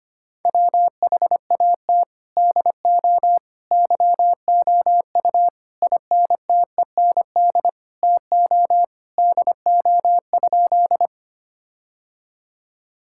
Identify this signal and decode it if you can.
Morse code, heard as a keyed tone, about 25 wpm, 700 Hz. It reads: WHAT DO YOU INTEND TO DO?
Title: Sentences from Top 1600 Words 25wpm